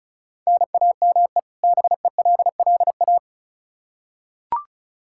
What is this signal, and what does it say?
Morse code, heard as a keyed tone, about 36 wpm, 700 Hz. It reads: NAME BELLA
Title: Practice QSO - Exchange 1 - Beginning - 35wpm